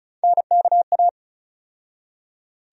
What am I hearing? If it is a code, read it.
Morse code, heard as a keyed tone, about 36 wpm, 700 Hz. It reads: NKA
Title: USA Call Sign Postfixes - 3 Letters 35wpm